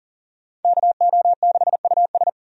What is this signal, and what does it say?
Morse code, heard as a keyed tone, about 40 wpm, 700 Hz. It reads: KO6US
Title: USA Call Signs - 5 Characters - 40wpm